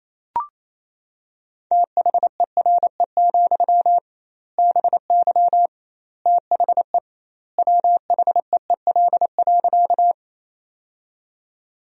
Morse code, heard as a keyed tone, about 28 wpm, 700 Hz. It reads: THERE, BY THE WHEEL.
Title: 2x Word-Spacing - Sentences from Top 500 Words - 28wpm